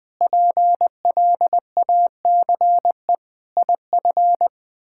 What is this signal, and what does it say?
Morse code, heard as a keyed tone, about 20 wpm, 700 Hz. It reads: PLACE IF
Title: Sets of 8 Words - 20wpm